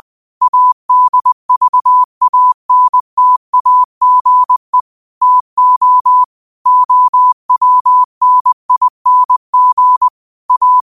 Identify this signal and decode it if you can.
Morse code, heard as a keyed tone, about 20 wpm, 1000 Hz. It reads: ADVANTAGE TO OWNING A
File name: qotdfiles / 20wpm